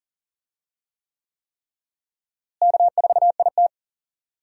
Morse code, heard as a keyed tone, about 40 wpm, 700 Hz. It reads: K4IT
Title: USA Call Signs - 4 Characters - Repeated 3x - 40wpm